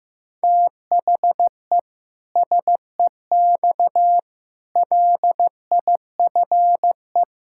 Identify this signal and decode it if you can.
Morse code, heard as a keyed tone, about 15 wpm, 700 Hz. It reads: THE SEX LIFE
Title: Pangram Challenge v2 - 15wpm